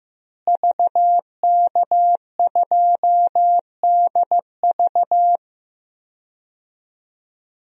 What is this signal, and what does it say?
Morse code, heard as a keyed tone, about 16 wpm, 700 Hz. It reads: VK2DV
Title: DXCC-100 Call Signs - 5 Characters - Repeated 2x - 15wpm